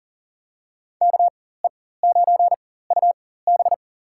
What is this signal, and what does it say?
Morse code, heard as a keyed tone, about 40 wpm, spaced as for 19 wpm, 700 Hz. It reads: KE9UB
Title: USA Call Signs - 5 Characters - 4x Character Spacing - 40wpm